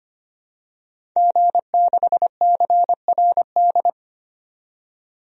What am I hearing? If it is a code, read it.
Morse code, heard as a keyed tone, about 25 wpm, 700 Hz. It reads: G6CRD
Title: DXCC-100 Call Signs - 5 Characters - 25wpm